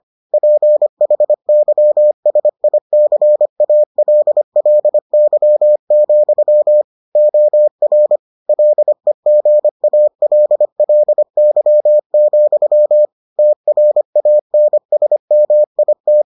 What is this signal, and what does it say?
Morse code, heard as a keyed tone, about 25 wpm, 600 Hz. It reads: PHYSICALLY, OR LEGALLY, TRANSMIT